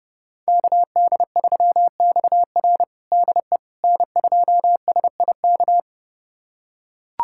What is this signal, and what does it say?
Morse code, heard as a keyed tone, about 30 wpm, 700 Hz. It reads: KD3XR DE N2SIK
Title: USA Callsign DE Callsign - 5 Characters - 30wpm